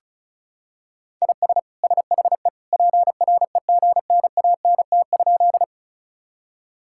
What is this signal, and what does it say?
Morse code, heard as a keyed tone, about 36 wpm, 700 Hz. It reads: IS SHE PREGNANT?
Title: Sentences from Top 1000 Words - Review Part 6 - 35wpm